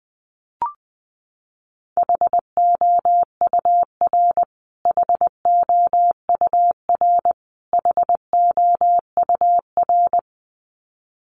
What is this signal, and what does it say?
Morse code, heard as a keyed tone, about 20 wpm, 700 Hz. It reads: HOUR HOUR HOUR
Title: Top 500 Words Repeated 20wpm